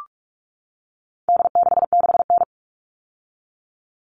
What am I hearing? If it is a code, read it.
Morse code, heard as a keyed tone, about 44 wpm, 700 Hz. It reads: D66N
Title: All Cut Numbers - 4-Digit 45wpm